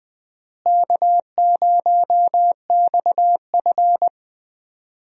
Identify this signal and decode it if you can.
Morse code, heard as a keyed tone, about 20 wpm, 700 Hz. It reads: K0XF
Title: USA Call Signs - 4 Characters - Repeated 2x - 20wpm